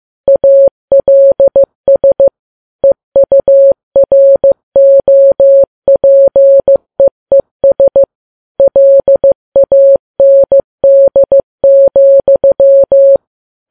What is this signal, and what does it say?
Morse code, heard as a keyed tone, about 15 wpm, 550 Hz. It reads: ALS EUROPEES LAND,
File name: Files Web tls / Morse on9fv / morse15kps